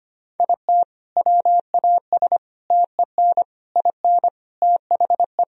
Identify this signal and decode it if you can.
Morse code, heard as a keyed tone, about 25 wpm, 700 Hz. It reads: IT WAS TEN IN THE
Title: Sentences from Top 1000 Words - Review Part 2 - 25wpm